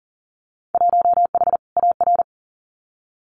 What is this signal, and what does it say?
Morse code, heard as a keyed tone, about 40 wpm, 700 Hz. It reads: 1H AR
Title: Winter Field Day Exchanges Encore 40wpm